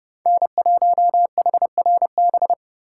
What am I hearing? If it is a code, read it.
Morse code, heard as a keyed tone, about 30 wpm, 700 Hz. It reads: N1HRB
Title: Speed Racing - USA 1x3 Call Signs 25wpm